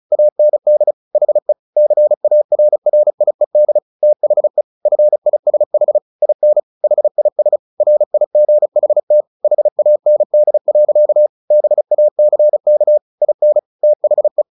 Morse code, heard as a keyed tone, about 34 wpm, 600 Hz. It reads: AND HE CARRIED THE FISH IN HIS RIGHT HAND. BACK IN THE
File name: the old man and the sea 2 35wpm